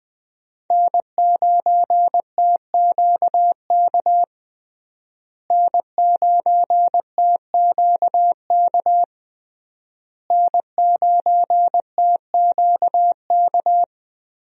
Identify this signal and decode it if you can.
Morse code, heard as a keyed tone, about 20 wpm, 700 Hz. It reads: N9TQK N9TQK N9TQK
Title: USA Call Signs - 5 Characters - Repeated 3x - 20wpm